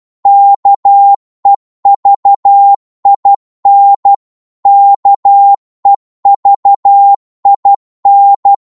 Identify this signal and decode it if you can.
Morse code, heard as a keyed tone, about 12 wpm, 800 Hz. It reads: KEVIN KEVIN
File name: Morse / qso-12wpm